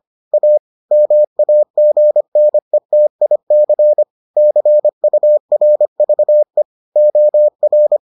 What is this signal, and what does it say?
Morse code, heard as a keyed tone, about 25 wpm, 600 Hz. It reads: A MAGNETIC CURVE OR